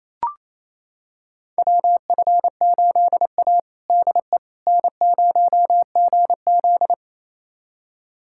Morse code, heard as a keyed tone, about 27 wpm, 700 Hz. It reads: WF8A DE N0GZ
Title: USA Callsign DE Callsign - 4 Characters - 28wpm